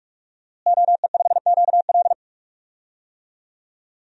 Morse code, heard as a keyed tone, about 46 wpm, 700 Hz. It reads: OE5QL
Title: DXCC-100 Call Signs - 5 Characters - Repeated 2x - 45wpm